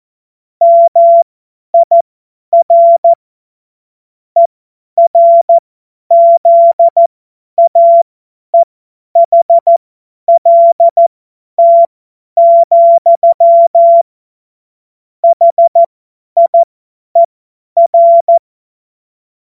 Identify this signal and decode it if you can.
Morse code, heard as a keyed tone, about 14 wpm, spaced as for 10 wpm, 680 Hz. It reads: MIR ERZAEHLT, HIER
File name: Morsekurs / Witze / Witz 28. März 2023 50 F70